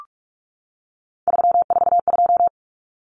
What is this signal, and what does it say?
Morse code, heard as a keyed tone, about 44 wpm, 700 Hz. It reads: 342
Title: Numbers - 3-Digit 45wpm